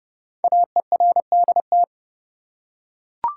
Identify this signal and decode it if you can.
Morse code, heard as a keyed tone, about 30 wpm, 700 Hz. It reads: AERDT
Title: Code Groups - 5 Letters - 30wpm